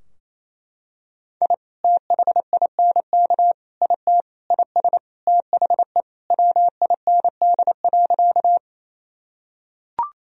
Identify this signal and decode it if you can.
Morse code, heard as a keyed tone, about 28 wpm, 700 Hz. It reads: I THINK IT IS THE WIND.
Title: Sentences from Top 1000 Words - Review Part 2 - 28wpm